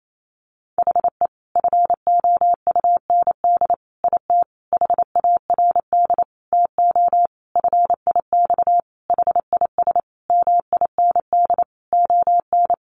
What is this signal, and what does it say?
Morse code, heard as a keyed tone, about 28 wpm, 700 Hz. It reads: HE FOUND IT HARD TO FIX HIS MIND ON
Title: Sentences from Top 1400 Words 28wpm